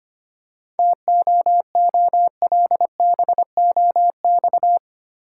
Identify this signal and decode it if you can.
Morse code, heard as a keyed tone, about 25 wpm, 690 Hz. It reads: TOOLBOX
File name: practice / 25wpm / Easy Words with Hints - 25wpm